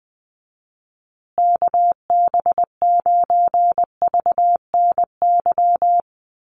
Morse code, heard as a keyed tone, about 20 wpm, 700 Hz. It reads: KB9VNY
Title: USA Call Signs Repeated 3x 20wpm